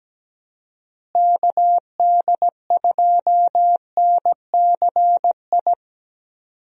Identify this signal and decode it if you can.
Morse code, heard as a keyed tone, about 17 wpm, 700 Hz. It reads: KD2NCI
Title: USA Call Signs - 6 Characters - Repeated 2x - 17wpm